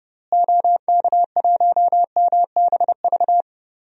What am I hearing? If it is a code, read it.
Morse code, heard as a keyed tone, about 32 wpm, 700 Hz. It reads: OK1MBV
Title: DXCC-100 Call Signs - 6 Characters - Repeated 3x - 30wpm